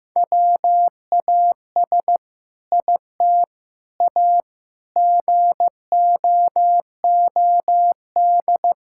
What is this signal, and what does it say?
Morse code, heard as a keyed tone, about 15 wpm, 700 Hz. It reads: WAS IT A GOOD
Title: Sentences from Top 1000 Words - Review Part 5 - 15wpm